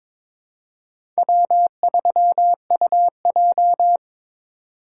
Morse code, heard as a keyed tone, about 21 wpm, 700 Hz. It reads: W3UJ
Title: USA Call Signs - 4 Characters - 3x Character Spacing - 22wpm